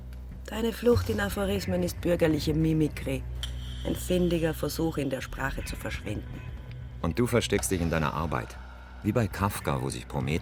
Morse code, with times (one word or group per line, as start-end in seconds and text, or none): none